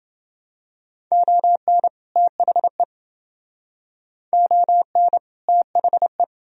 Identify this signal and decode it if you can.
Morse code, heard as a keyed tone, about 28 wpm, 700 Hz. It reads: ON THE ON THE